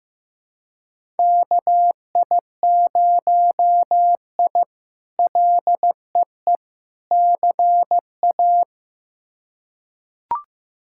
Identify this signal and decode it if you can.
Morse code, heard as a keyed tone, about 15 wpm, 700 Hz. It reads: KI0I LEE CA
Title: K1USN SST QSO Practice - 15wpm